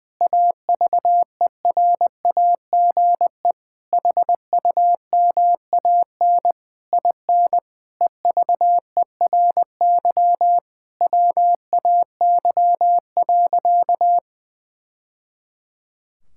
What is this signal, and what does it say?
Morse code, heard as a keyed tone, about 21 wpm, 700 Hz. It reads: AVERAGE HUMAN IN EVERY WAY.
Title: Sentences from Top 1000 Words - Review Part 6 - 20wpm